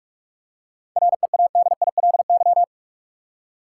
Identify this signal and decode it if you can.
Morse code, heard as a keyed tone, about 46 wpm, 700 Hz. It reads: READILY
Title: Seven Letter Words - 45wpm